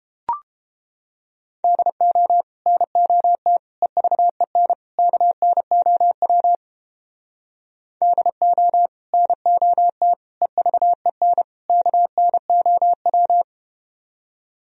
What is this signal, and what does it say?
Morse code, heard as a keyed tone, about 32 wpm, 700 Hz. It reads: DO NOT EVEN KNOW DO NOT EVEN KNOW